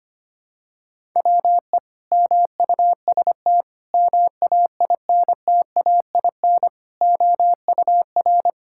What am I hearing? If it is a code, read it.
Morse code, heard as a keyed tone, about 25 wpm, 700 Hz. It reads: WE MUST MAINTAIN OUR
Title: Sentences from Top 1000 Words - Review Part 5 - 25wpm